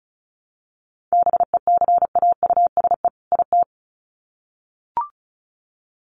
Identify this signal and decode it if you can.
Morse code, heard as a keyed tone, about 34 wpm, 700 Hz. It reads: BECAUSE IT
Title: Sets of 2 Words 35wpm